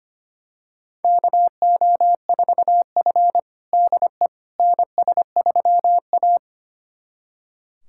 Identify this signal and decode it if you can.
Morse code, heard as a keyed tone, about 25 wpm, 700 Hz. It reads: KO4F DE NS3A
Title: USA Callsign DE Callsign - 4 Characters - 25wpm